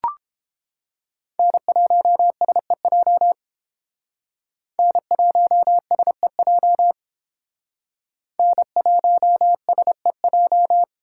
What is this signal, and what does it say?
Morse code, heard as a keyed tone, about 29 wpm, 700 Hz. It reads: N1SEJ N1SEJ N1SEJ